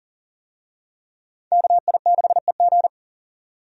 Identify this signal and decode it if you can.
Morse code, heard as a keyed tone, about 40 wpm, 700 Hz. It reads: KI6EG
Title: USA Call Signs - 5 Characters - 40wpm